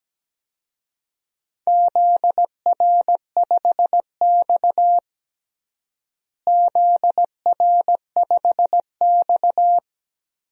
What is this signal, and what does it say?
Morse code, heard as a keyed tone, about 17 wpm, 700 Hz. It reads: ZR5X ZR5X
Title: DXCC-100 Call Signs - 4 Characters - Repeated 3x - 17wpm